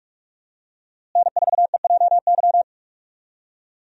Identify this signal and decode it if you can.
Morse code, heard as a keyed tone, about 43 wpm, 700 Hz. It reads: N4EJY
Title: USA Call Signs - 5 Characters - Repeated 3x - 45wpm